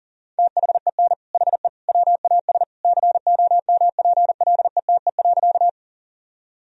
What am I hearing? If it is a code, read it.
Morse code, heard as a keyed tone, about 40 wpm, 700 Hz. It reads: THEN HE WAS COMPLETE.
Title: Sentences from Top 1000 Words - Review Part 2 - 40wpm